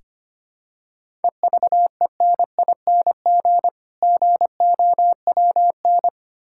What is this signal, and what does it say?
Morse code, heard as a keyed tone, about 25 wpm, 690 Hz. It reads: EVENING GOWN
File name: practice / 25wpm / Difficult Words with Hints - 25wpm